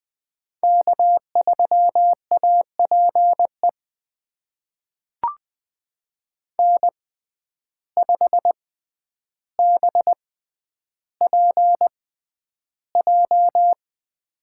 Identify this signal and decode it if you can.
Morse code, heard as a keyed tone, about 20 wpm, spaced as for 10 wpm, 700 Hz. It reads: K3APE N5BPJ
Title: USA Call Signs - 5 Characters - 6x Character Spacing - 20wpm